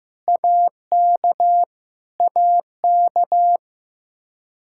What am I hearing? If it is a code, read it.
Morse code, heard as a keyed tone, about 15 wpm, 700 Hz. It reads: AK AK
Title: Practice QSO - Exchange 1 - Beginning - 15wpm